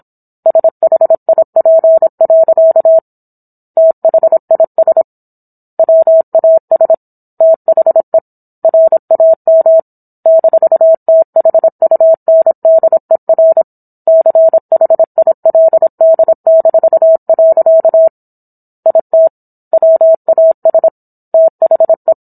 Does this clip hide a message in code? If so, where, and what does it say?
Yes: Morse code from 0.5 to 22.1 s, heard as a keyed tone, about 27 wpm, 650 Hz. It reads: SHIP. THIS WAS THE RAM -THUNDER CHILD-. IT WAS THE